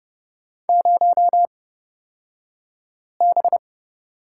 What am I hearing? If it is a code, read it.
Morse code, heard as a keyed tone, about 30 wpm, spaced as for 5 wpm, 700 Hz. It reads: 0B
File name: Lesson 37d - Letter x - Two Characters 30wpm